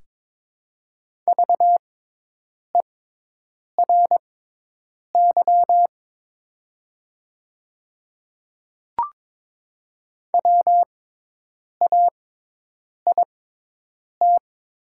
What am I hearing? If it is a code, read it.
Morse code, heard as a keyed tone, about 22 wpm, spaced as for 8 wpm, 700 Hz. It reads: VERY WAIT
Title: 6x Character Spacing - Top 500 Words - 22wpm